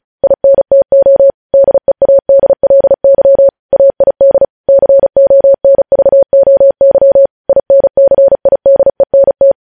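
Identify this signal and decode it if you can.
Morse code, heard as a keyed tone, about 36 wpm, 550 Hz. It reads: INTO DEADLY AID CONVOY INCIDENT
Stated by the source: Morse code practice sending news headlines at 35 wpm.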